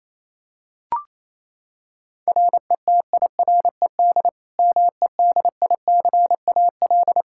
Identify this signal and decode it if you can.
Morse code, heard as a keyed tone, about 28 wpm, 700 Hz. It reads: RETIRED MEDICAL
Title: K1USN SST QSO Practice 28wpm